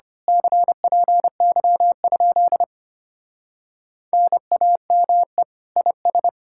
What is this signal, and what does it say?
Morse code, heard as a keyed tone, about 24 wpm, 700 Hz. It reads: CPY? NAME IS
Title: Speed-Racing Common QSO Phrases - 25wpm